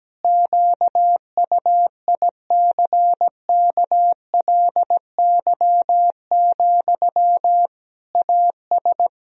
Morse code, heard as a keyed tone, about 17 wpm, 700 Hz. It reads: QUICKLY, AS
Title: Sentences from Top 1000 Words - Review Part 2 - 17wpm